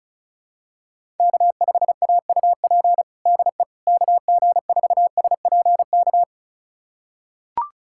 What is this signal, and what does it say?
Morse code, heard as a keyed tone, about 34 wpm, 700 Hz. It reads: K5AUP DE KG4SPK
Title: USA Callsign DE Callsign 35wpm